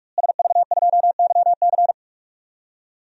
Morse code, heard as a keyed tone, about 46 wpm, 700 Hz. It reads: SV2YC
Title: DXCC-100 Call Signs - 5 Characters - Repeated 3x - 45wpm